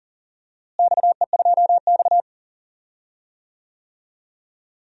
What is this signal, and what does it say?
Morse code, heard as a keyed tone, about 40 wpm, 700 Hz. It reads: XE2X
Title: DXCC-100 Call Signs - 4 Characters - Repeated 2x - 40wpm